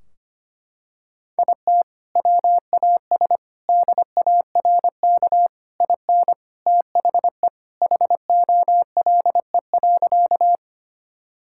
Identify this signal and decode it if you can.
Morse code, heard as a keyed tone, about 25 wpm, 700 Hz. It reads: IT WAS DARK IN THE HOLE.
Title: Sentences from Top 1600 Words 25wpm